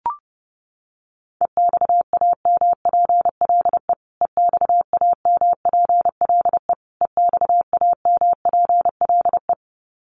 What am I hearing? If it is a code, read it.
Morse code, heard as a keyed tone, about 30 wpm, 700 Hz. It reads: EXAMPLE EXAMPLE EXAMPLE
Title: Top 500 Words Repeated 30wpm